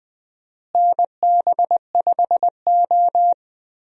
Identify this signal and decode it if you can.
Morse code, heard as a keyed tone, about 21 wpm, 700 Hz. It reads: NB5O
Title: USA Call Signs - 4 Characters - Repeated 2x - 20wpm